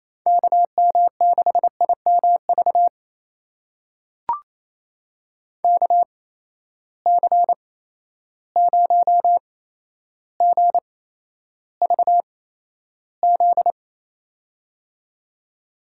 Morse code, heard as a keyed tone, about 28 wpm, spaced as for 12 wpm, 700 Hz. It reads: KM6IMV KC0GVZ
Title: USA Call Signs - 6 Characters - 8x Character Spacing - 28wpm